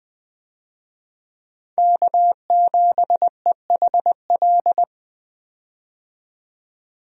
Morse code, heard as a keyed tone, about 20 wpm, 700 Hz. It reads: K7EHL
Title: USA Call Signs - 5 Characters - 20wpm